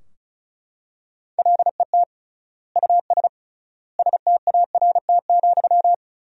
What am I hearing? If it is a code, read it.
Morse code, heard as a keyed tone, about 34 wpm, 700 Hz. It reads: LET US START,
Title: 3x Word-Spacing - Sentences from Top 300 Words - 35wpm